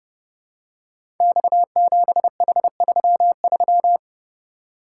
Morse code, heard as a keyed tone, about 30 wpm, 700 Hz. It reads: X7H33
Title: Code Groups - 5 Letter:Numbers - 30wpm